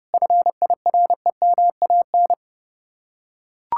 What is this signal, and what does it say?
Morse code, heard as a keyed tone, about 30 wpm, 700 Hz. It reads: FIREMAN